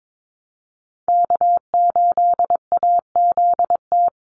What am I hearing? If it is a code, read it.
Morse code, heard as a keyed tone, about 22 wpm, 700 Hz. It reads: K8AZT
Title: Call Signs Repeated 3x - 22wpm